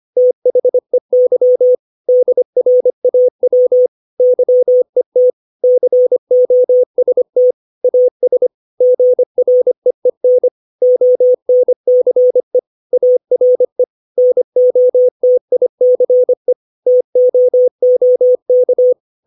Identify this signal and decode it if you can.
Morse code, heard as a keyed tone, about 25 wpm, 500 Hz. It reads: THEY DRAW YET COST AS GREEN ONCE ARE NOTICE TOOK
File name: 632 words / 632 Words 25WPM